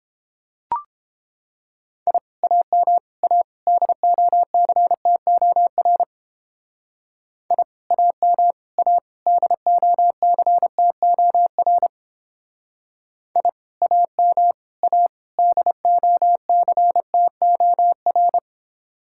Morse code, heard as a keyed tone, about 29 wpm, 700 Hz. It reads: I AM A DOCTOR I AM A DOCTOR I AM A DOCTOR